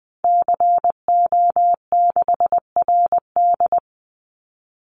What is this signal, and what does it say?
Morse code, heard as a keyed tone, about 20 wpm, 700 Hz. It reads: CO6RD
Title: Call Signs - 20wpm